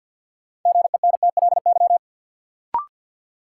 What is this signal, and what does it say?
Morse code, heard as a keyed tone, about 48 wpm, 700 Hz. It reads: GENTLY
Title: Sentences from Top 1500 Words - New Word Drill Alt - 50wpm